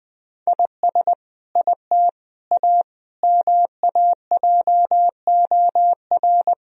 Morse code, heard as a keyed tone, about 20 wpm, 700 Hz. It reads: IS IT A MAJOR
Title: Sentences from Top 1000 Words - Review Part 4 - 20wpm